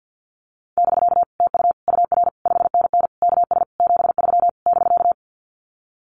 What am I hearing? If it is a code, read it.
Morse code, heard as a keyed tone, about 48 wpm, 700 Hz. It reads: <BK> TU UR 5NN KS 73 <BK>
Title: Common QSO Phrases Encore 50wpm